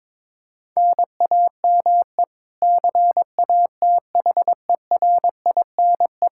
Morse code, heard as a keyed tone, about 22 wpm, 700 Hz. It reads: NAME CATHERINE
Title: Practice QSO - Exchange 1 - Beginning - 22wpm